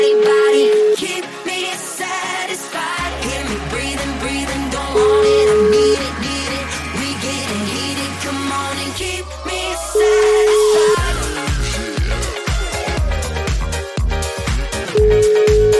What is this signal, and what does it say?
Ringback tone (Europe, 425 Hz), with noise.